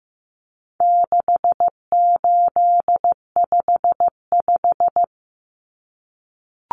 Morse code, heard as a keyed tone, about 15 wpm, 700 Hz. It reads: 6855
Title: Common Cut Numbers - 4-Digit 15wpm